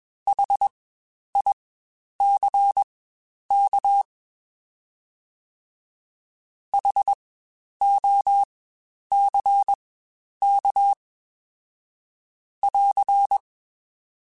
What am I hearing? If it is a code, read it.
Morse code, heard as a keyed tone, about 20 wpm, spaced as for 10 wpm, 800 Hz. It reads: HICK HOCK <AR>
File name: Spaced Alphas / Spaced Alphas mp3s / Hs sp @20